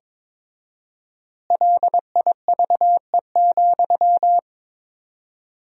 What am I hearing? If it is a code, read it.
Morse code, heard as a keyed tone, about 22 wpm, 700 Hz. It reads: LIVE,
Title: 4x Word-Spacing - Sentences from Top 200 Words - 22wpm